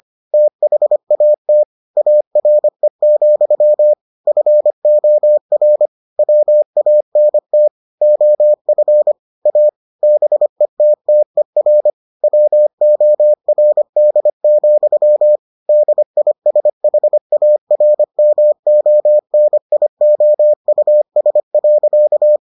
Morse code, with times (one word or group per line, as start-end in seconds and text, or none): 0.3-1.6 THAT
2.0-3.9 ARE,
4.3-5.9 FOR
6.2-7.7 WANT
8.0-9.1 OF
9.5-9.7 A
10.0-11.9 BETTER
12.2-15.4 WORD,
15.7-22.4 DISHARMONIOUS.